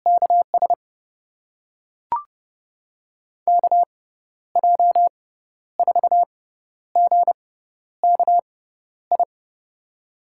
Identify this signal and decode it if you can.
Morse code, heard as a keyed tone, about 30 wpm, spaced as for 12 wpm, 700 Hz. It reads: KS KJ4GKI
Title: USA Call Signs - 6 Characters - 6x Character Spacing - 30wpm